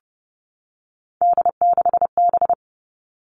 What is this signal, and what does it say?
Morse code, heard as a keyed tone, about 30 wpm, 700 Hz. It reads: D6B
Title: Three Challenging Letter-Number Combinations 30wpm